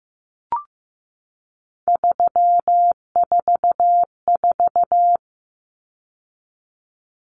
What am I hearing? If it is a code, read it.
Morse code, heard as a keyed tone, about 15 wpm, 700 Hz. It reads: 344
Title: Numbers - 3-Digit 15wpm